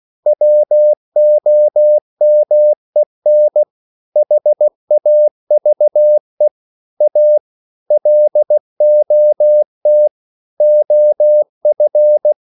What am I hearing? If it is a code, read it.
Morse code, heard as a keyed tone, about 16 wpm, 600 Hz. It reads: WOMEN HAVE A LOT OF